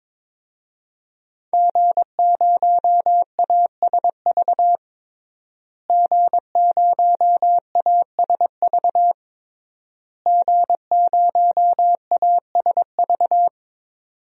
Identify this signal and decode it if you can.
Morse code, heard as a keyed tone, about 22 wpm, 700 Hz. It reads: G0ASV G0ASV G0ASV
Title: DXCC-100 Call Signs - 5 Characters - Repeated 3x - 22wpm